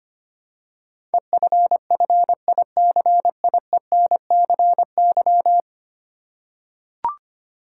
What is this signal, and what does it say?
Morse code, heard as a keyed tone, about 25 wpm, 700 Hz. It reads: EFFICIENCY
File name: Ten Letter Words - 25wpm